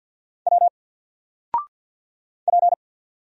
Morse code, heard as a keyed tone, about 50 wpm, 700 Hz. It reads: W P
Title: Single Letter-Number 50wpm